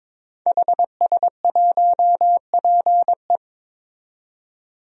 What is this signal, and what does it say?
Morse code, heard as a keyed tone, about 22 wpm, 700 Hz. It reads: HS1PE
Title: DXCC-100 Call Signs - 5 Characters - 22wpm